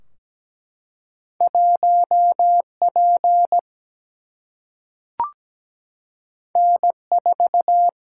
Morse code, heard as a keyed tone, about 17 wpm, 700 Hz. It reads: 1P N4
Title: Single-Letter Single-Number 17wpm